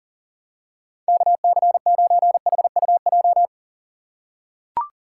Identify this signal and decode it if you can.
Morse code, heard as a keyed tone, about 40 wpm, 700 Hz. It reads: KC9HUJ
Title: USA Call Signs - 2x Character Spacing - 40wpm